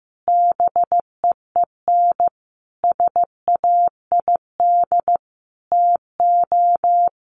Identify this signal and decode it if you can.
Morse code, heard as a keyed tone, about 15 wpm, 700 Hz. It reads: BEEN SAID TO
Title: Sentences from Top 1400 Words 15wpm